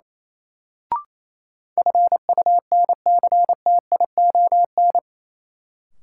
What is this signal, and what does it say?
Morse code, heard as a keyed tone, about 28 wpm, 700 Hz. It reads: FUNCTION
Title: Sentences from Top 1100 Words - New Word Drill Alt - 28wpm